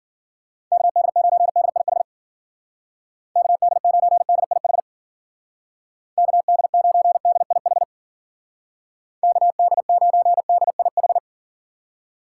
Morse code, heard as a keyed tone, about 49 wpm, 700 Hz. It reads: KD9DIH KD9DIH KD9DIH KD9DIH
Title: Speed Racing - USA 2x3 Call Signs 40wpm